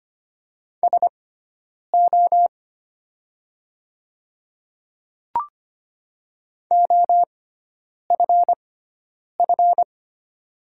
Morse code, heard as a keyed tone, about 25 wpm, spaced as for 9 wpm, 700 Hz. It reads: SO OFF
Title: 6x Character Spacing - Top 200 Words - 25wpm